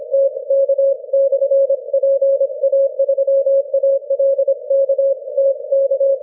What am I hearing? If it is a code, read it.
Morse code, heard as a keyed tone, about 25 wpm, 550 Hz. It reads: TK/PA3ALK TK
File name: wp / tk pa3alk 6m cw 20 may 2012